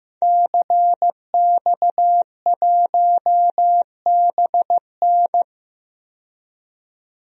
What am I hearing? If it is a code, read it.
Morse code, heard as a keyed tone, about 15 wpm, 700 Hz. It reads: CX1BN
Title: DXCC-100 Call Signs - 5 Characters - 15wpm